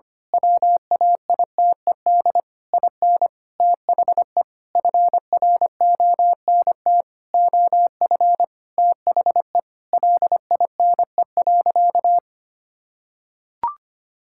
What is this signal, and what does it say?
Morse code, heard as a keyed tone, about 25 wpm, 700 Hz. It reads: WAITED IN THE FRONT OF THE LINE.
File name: practice / 25wpm / Sentences with 4-Word Phrases - 25wpm